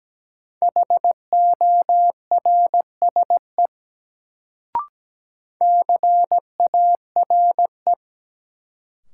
Morse code, heard as a keyed tone, about 17 wpm, 700 Hz. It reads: HORSE CARE
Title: Sentences from Top 300 Words - New Word Drill - 17wpm